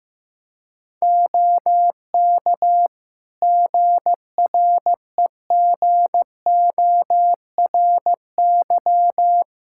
Morse code, heard as a keyed tone, about 15 wpm, 700 Hz. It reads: OK GREGORY
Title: Practice QSO - Exchange 3 - More Info - 15wpm